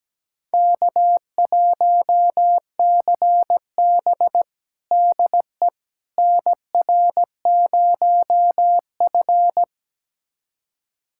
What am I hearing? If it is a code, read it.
Morse code, heard as a keyed tone, about 17 wpm, 700 Hz. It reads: K1CB DE NR0F
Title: USA Callsign DE Callsign - 4 Characters - 17wpm